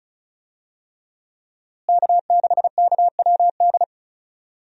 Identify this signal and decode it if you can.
Morse code, heard as a keyed tone, about 34 wpm, 700 Hz. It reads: K6KWD
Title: USA Call Signs - 5 Characters - Repeated 3x - 35wpm